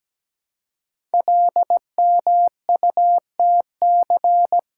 Morse code, heard as a keyed tone, about 17 wpm, 700 Hz. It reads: LMUTC
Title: Code Groups - 5 Letters - 17wpm